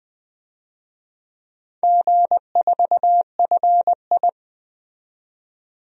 Morse code, heard as a keyed tone, about 20 wpm, 700 Hz. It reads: G4FI
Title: DXCC-100 Call Signs - 4 Characters - 20wpm